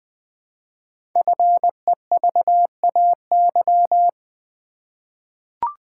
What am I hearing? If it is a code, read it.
Morse code, heard as a keyed tone, about 20 wpm, 700 Hz. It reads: FEVAY
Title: Code Groups - 5 Letter:Numbers - 20wpm